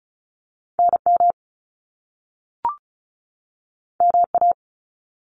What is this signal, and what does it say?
Morse code, heard as a keyed tone, about 36 wpm, 700 Hz. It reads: NM MA
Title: US States 35wpm